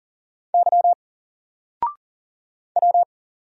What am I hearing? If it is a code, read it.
Morse code, heard as a keyed tone, about 40 wpm, 700 Hz. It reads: Y W
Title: Single Letter-Number Alt 40wpm